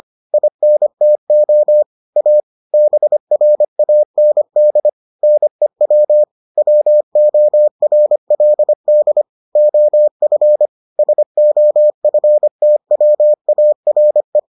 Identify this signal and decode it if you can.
Morse code, heard as a keyed tone, about 25 wpm, 600 Hz. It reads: INTO A BRAND NEW WORLD OF SOFTWARE